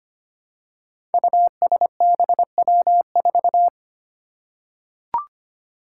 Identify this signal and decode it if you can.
Morse code, heard as a keyed tone, about 25 wpm, 700 Hz. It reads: USBW4
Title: Five Challenging Letter-Number Combinations 25wpm